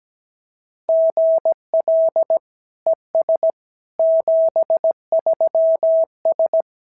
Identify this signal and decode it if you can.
Morse code, heard as a keyed tone, about 17 wpm, 650 Hz. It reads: GL ES 73S
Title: SOTA QSO Practice 17wpm